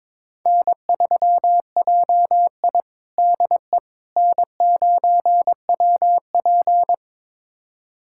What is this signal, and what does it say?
Morse code, heard as a keyed tone, about 22 wpm, 700 Hz. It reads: N3JI DE N9WP
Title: USA Callsign DE Callsign - 4 Characters - 22wpm